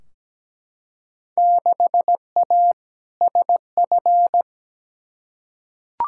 Morse code, heard as a keyed tone, about 17 wpm, 700 Hz. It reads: 6A SF
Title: Field Day Exchanges - 17wpm